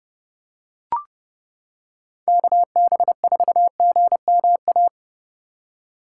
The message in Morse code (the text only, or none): KB4GMA